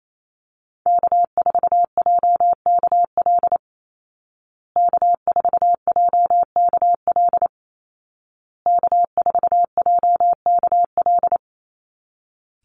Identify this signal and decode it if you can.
Morse code, heard as a keyed tone, about 28 wpm, 700 Hz. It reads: K4JKL K4JKL K4JKL
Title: Call Signs Repeated 2x - 28wpm